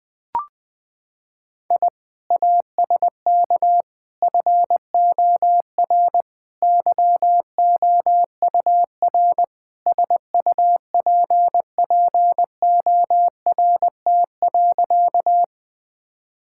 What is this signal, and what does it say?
Morse code, heard as a keyed tone, about 21 wpm, 700 Hz. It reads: I ASK FOR YOUR SUPPORT.
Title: Sentences from Top 1000 Words - Review Part 5 - 20wpm